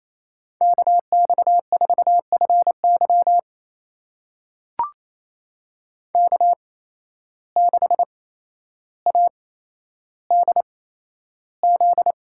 Morse code, heard as a keyed tone, about 28 wpm, spaced as for 12 wpm, 700 Hz. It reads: KX4FY K6ADZ
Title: USA Call Signs - 8x Character Spacing - 28wpm